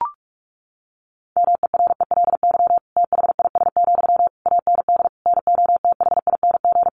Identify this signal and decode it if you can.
Morse code, heard as a keyed tone, about 44 wpm, 700 Hz. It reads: MERELY THIS, AND NOTHING
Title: Sentences from Top 1400 Words 45wpm